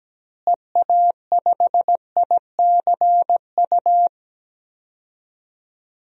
Morse code, heard as a keyed tone, about 17 wpm, 700 Hz. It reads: EA5ICU
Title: DXCC-100 Call Signs - 6 Characters - Repeated 3x - 17wpm